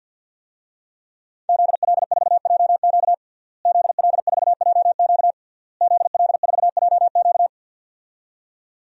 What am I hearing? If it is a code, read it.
Morse code, heard as a keyed tone, about 52 wpm, 700 Hz. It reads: ZL4JQ ZL4JQ ZL4JQ
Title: DXCC-100 Call Signs - 5 Characters - Repeated 3x - 50wpm